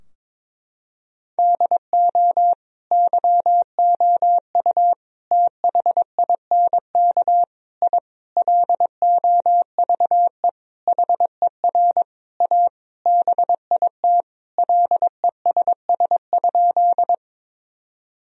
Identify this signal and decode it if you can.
Morse code, heard as a keyed tone, about 22 wpm, 700 Hz. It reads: DO YOU THINK I LOVE HER A BIT LESS?
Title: Sentences from Top 1000 Words - Review Part 2 - 22wpm